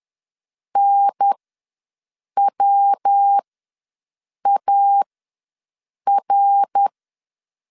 Morse code, heard as a keyed tone, about 10 wpm, spaced as for 6 wpm, 800 Hz. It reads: NWAR
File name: LEZIONI / 6 PKX0 / 6 PKX ws 10wpm 3706